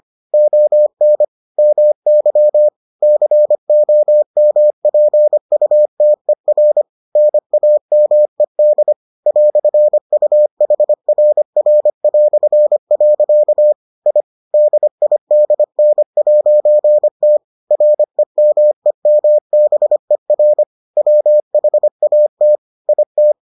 Morse code, heard as a keyed tone, about 25 wpm, 600 Hz. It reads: ON MY COMPUTER NAMED "UHRR". I DIDN'T REMEMBER WHAT IT